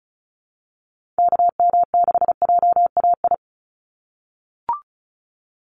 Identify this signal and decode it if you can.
Morse code, heard as a keyed tone, about 36 wpm, 700 Hz. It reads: KM6JAI
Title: USA Call Signs 35wpm